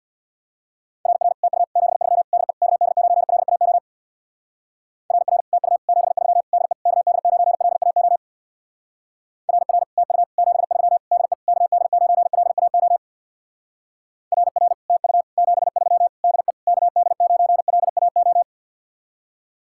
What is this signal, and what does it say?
Morse code, heard as a keyed tone, about 61 wpm, 700 Hz. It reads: RR TU 73 DE KD9LAO RR TU 73 DE KD9LAO RR TU 73 DE KD9LAO RR TU 73 DE KD9LAO
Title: Speed-Racing Common QSO Phrases - 50wpm